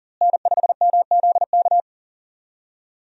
Morse code, heard as a keyed tone, about 40 wpm, 700 Hz. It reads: N5MZK
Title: USA Call Signs - 5 Characters - Repeated 2x - 40wpm